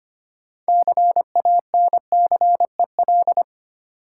Morse code, heard as a keyed tone, about 25 wpm, 700 Hz. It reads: CANCEL
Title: Sentences from Top 1200 Words - New Word Drill - 25wpm